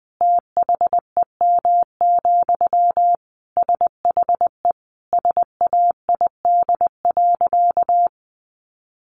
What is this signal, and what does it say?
Morse code, heard as a keyed tone, about 20 wpm, 700 Hz. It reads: THEM, SHE SAID.
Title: Sentences from Top 1400 Words 20wpm